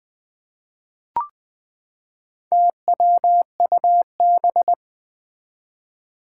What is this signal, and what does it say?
Morse code, heard as a keyed tone, about 20 wpm, 700 Hz. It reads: TWUB
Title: Four Challenging Letter-Number Combinations 20wpm